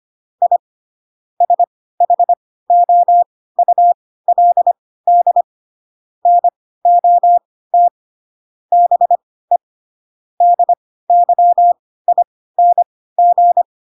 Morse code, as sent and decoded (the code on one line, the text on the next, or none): .. / ... .... --- ..- .-.. -.. / -. --- - / -... / -.. -.-- .. -. --.
I SHOULD NOT BE DYING